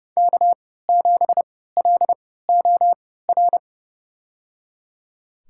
Morse code, heard as a keyed tone, about 30 wpm, spaced as for 17 wpm, 700 Hz. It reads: K7LOR
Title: USA Call Signs - 5 Characters - 3x Character Spacing - 30wpm